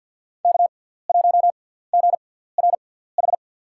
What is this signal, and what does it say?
Morse code, heard as a keyed tone, about 48 wpm, spaced as for 17 wpm, 700 Hz. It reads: K1GRH
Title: USA Call Signs - 5 Characters - 6x Character Spacing - 50wpm